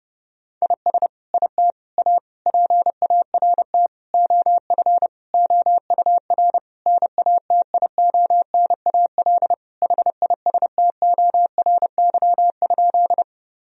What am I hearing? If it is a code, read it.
Morse code, heard as a keyed tone, about 30 wpm, 700 Hz. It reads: IS IT A PART OF OUR NATIONAL HISTORY?
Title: Sentences from Top 1000 Words - Review Part 4 - 30wpm